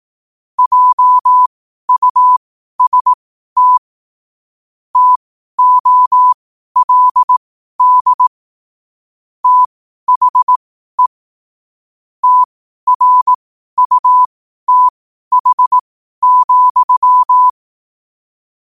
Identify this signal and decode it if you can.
Morse code, heard as a keyed tone, about 18 wpm, spaced as for 12 wpm, 1000 Hz. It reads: JUST TOLD THE TRUTH,
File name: qotdfiles / 12wpm